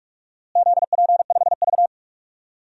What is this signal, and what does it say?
Morse code, heard as a keyed tone, about 44 wpm, 700 Hz. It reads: ZP5V
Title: DXCC-100 Call Signs - 4 Characters - Repeated 3x - 45wpm